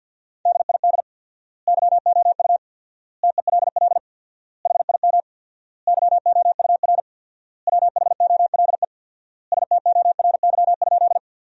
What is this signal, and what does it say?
Morse code, heard as a keyed tone, about 50 wpm, 700 Hz. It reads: DID YOU TELL HIM YOUR WHOLE STORY?
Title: 4x Word-Spacing - Sentences from Top 400 Words - 50wpm